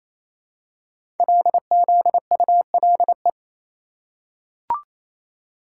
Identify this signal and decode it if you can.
Morse code, heard as a keyed tone, about 29 wpm, 700 Hz. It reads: LZULE
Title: Code Groups - 5 Letters - 28wpm